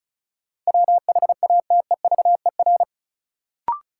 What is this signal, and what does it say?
Morse code, heard as a keyed tone, about 36 wpm, 700 Hz. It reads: WHATEVER
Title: Sentences from Top 1100 Words - New Word Drill Alt - 35wpm